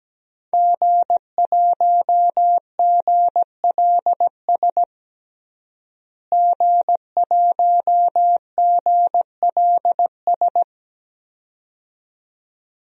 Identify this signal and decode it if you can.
Morse code, heard as a keyed tone, about 17 wpm, 700 Hz. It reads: G1GLS G1GLS
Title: DXCC-100 Call Signs - 5 Characters - Repeated 2x - 17wpm